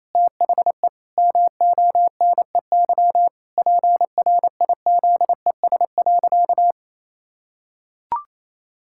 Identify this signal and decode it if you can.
Morse code, heard as a keyed tone, about 28 wpm, 700 Hz. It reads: THE MONEY PRIZES.
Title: Pangram Challenge v2 - 28wpm